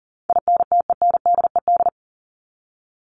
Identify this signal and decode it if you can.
Morse code, heard as a keyed tone, about 40 wpm, 700 Hz. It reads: INTENDED
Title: ED Suffix - Top 100 Words with Base Words 40wpm